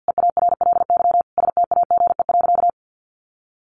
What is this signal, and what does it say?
Morse code, heard as a keyed tone, about 50 wpm, 700 Hz. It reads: EARLY STAGE.
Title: Sentences from Top 1200 Words 50wpm